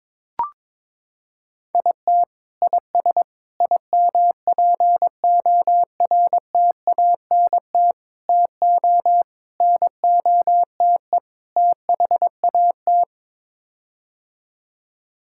Morse code, heard as a keyed tone, about 22 wpm, 700 Hz. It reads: IT IS IMPORTANT TO NOTE THAT
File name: practice / 22wpm / Sets of 6 Words - 22wpm